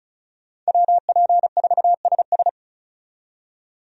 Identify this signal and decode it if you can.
Morse code, heard as a keyed tone, about 36 wpm, 700 Hz. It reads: WP4SS
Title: USA Call Signs - 5 Characters - 35wpm